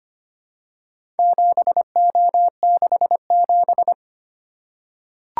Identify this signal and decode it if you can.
Morse code, heard as a keyed tone, about 25 wpm, 700 Hz. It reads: 7O67
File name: practice / 25wpm / Four Letter-Number Combinations - 25wpm